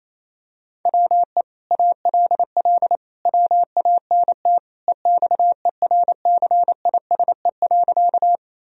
Morse code, heard as a keyed tone, about 28 wpm, 700 Hz. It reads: WE ALL WANT EXERCISE.
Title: Sentences from Top 1000 Words - Review Part 4 - 28wpm